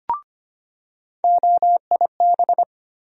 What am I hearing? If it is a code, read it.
Morse code, heard as a keyed tone, about 25 wpm, 700 Hz. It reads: OIB
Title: USA Call Sign Postfixes - 3 Letters 25wpm